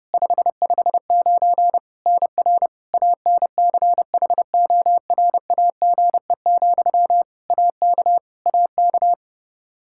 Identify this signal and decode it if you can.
Morse code, heard as a keyed tone, about 32 wpm, 700 Hz. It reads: 559 NR ANCHORAGE, AK AK
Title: Practice QSO - Exchange 1 - Beginning - 30wpm